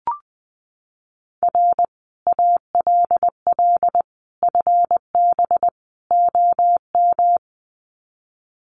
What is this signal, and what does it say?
Morse code, heard as a keyed tone, about 20 wpm, 700 Hz. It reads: R ALL FB OM
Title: Common QSO Phrases 20wpm